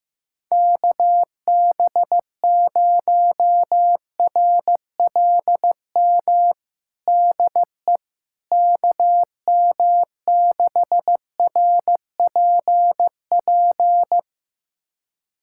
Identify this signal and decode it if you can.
Morse code, heard as a keyed tone, about 15 wpm, 700 Hz. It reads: KB0RLM DE KM6RPP
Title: USA Callsign DE Callsign - 6 Characters - 15wpm